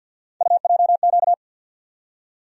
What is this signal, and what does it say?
Morse code, heard as a keyed tone, about 53 wpm, 700 Hz. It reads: UJQ
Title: USA Call Sign Postfixes - 3 Letters 50wpm